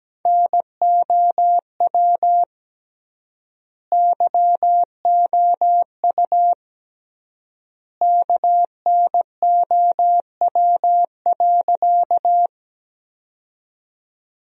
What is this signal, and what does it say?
Morse code, heard as a keyed tone, about 17 wpm, 700 Hz. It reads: NOW YOU KNOW.
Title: 3x Word-Spacing - Sentences from Top 100 Words - 17wpm